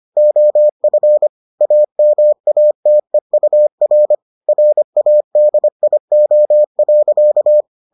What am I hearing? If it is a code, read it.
Morse code, heard as a keyed tone, about 25 wpm, 600 Hz. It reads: OF AMATEUR RADIO.